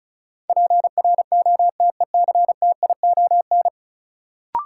Morse code, heard as a keyed tone, about 36 wpm, 700 Hz. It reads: PROTECTION
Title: Sentences from Top 1600 Words - New Word Drill Alt - 35wpm